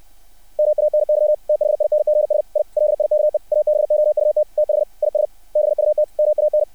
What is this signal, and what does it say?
Morse code, heard as a keyed tone, about 60 wpm, 600 Hz. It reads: MANY THINGS I LIKE ABOUT THIS BUT MAN
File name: archived / CRSnet